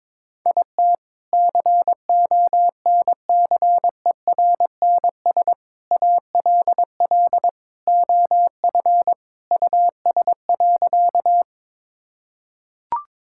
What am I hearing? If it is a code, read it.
Morse code, heard as a keyed tone, about 22 wpm, 700 Hz. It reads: IT CONCERNS ALL OF US.
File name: practice / 22wpm / Sentences with 3-Word Phrases - 22wpm